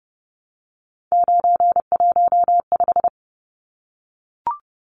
Morse code, heard as a keyed tone, about 28 wpm, 700 Hz. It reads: 915
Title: Numbers - 3-Digit 30wpm